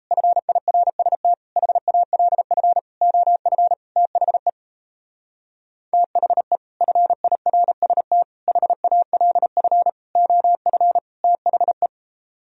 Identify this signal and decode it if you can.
Morse code, heard as a keyed tone, about 36 wpm, 700 Hz. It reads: FIRST HALF OF THE THE FIRST HALF OF THE